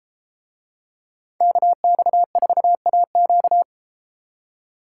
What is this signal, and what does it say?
Morse code, heard as a keyed tone, about 32 wpm, 700 Hz. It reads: KX4AQ